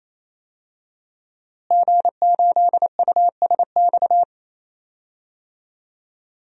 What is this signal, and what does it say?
Morse code, heard as a keyed tone, about 28 wpm, 700 Hz. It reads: G8USX
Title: DXCC-100 Call Signs - 5 Characters - Repeated 3x - 28wpm